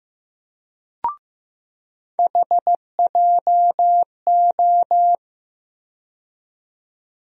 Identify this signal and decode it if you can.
Morse code, heard as a keyed tone, about 15 wpm, 700 Hz. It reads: HJO
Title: USA Call Sign Postfixes - 3 Letters 15wpm